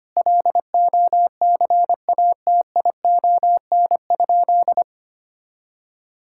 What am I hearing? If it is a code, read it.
Morse code, heard as a keyed tone, about 25 wpm, 700 Hz. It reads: LOCATION?
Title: Sentences from Top 1000 Words - Review Part 6 - 25wpm